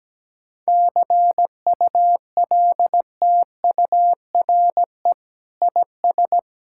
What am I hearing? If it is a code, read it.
Morse code, heard as a keyed tone, about 17 wpm, 700 Hz. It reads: CULTURE IS
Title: Sentences from Top 1000 Words - Review Part 6 - 17wpm